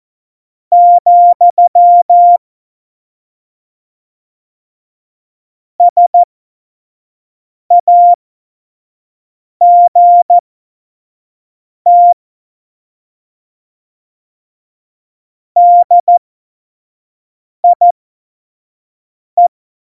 Morse code, heard as a keyed tone, about 14 wpm, spaced as for 5 wpm, 700 Hz. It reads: , SAGT DIE
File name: Morsekurs / Witze / Witz 16. Oktober 2018, 25 F70